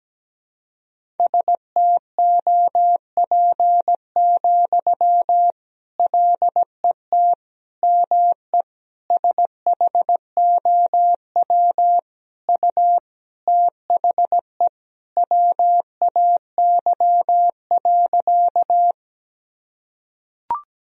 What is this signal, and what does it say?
Morse code, heard as a keyed tone, about 17 wpm, 700 Hz. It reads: STOP, LET ME SHOW U THE WAY.
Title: Sentences from Top 300 Words and Abbreviations 17wpm